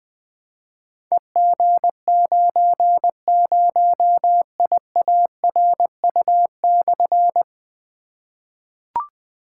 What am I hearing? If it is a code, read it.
Morse code, heard as a keyed tone, about 20 wpm, 700 Hz. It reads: EG90IARU/
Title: Difficult Call Signs 20wpm